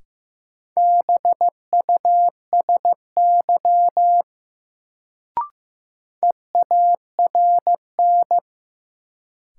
Morse code, heard as a keyed tone, about 15 wpm, 700 Hz. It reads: BUSY EARN
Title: Sentences from Top 1500 Words - New Word Drill Alt - 15wpm